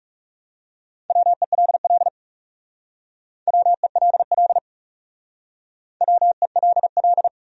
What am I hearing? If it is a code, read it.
Morse code, heard as a keyed tone, about 40 wpm, 700 Hz. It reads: WELL WELL WELL